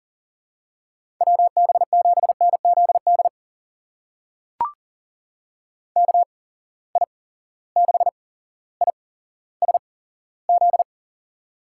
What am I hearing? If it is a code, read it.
Morse code, heard as a keyed tone, about 39 wpm, spaced as for 17 wpm, 700 Hz. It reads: WB7NZD KI6ISZ
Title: USA Call Signs - 6 Characters - 8x Character Spacing - 40wpm